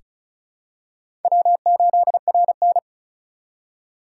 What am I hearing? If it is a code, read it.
Morse code, heard as a keyed tone, about 36 wpm, 700 Hz. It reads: W8RN
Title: USA Call Signs - 4 Characters - 4x Character Spacing - 35wpm